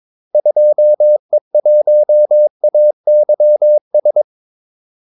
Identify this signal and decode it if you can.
Morse code, heard as a keyed tone, about 22 wpm, 600 Hz. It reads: 2E1AYS
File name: Callsigns-22wpm